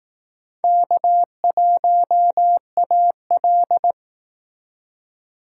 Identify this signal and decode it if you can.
Morse code, heard as a keyed tone, about 18 wpm, 700 Hz. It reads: K1AL
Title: Speed Racing - USA 1x2 Call Signs 15wpm